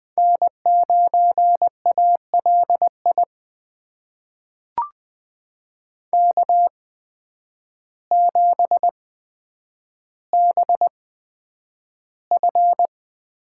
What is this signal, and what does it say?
Morse code, heard as a keyed tone, about 20 wpm, spaced as for 9 wpm, 700 Hz. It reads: N9ALI K7BF
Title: USA Call Signs - 5 Characters - 8x Character Spacing - 20wpm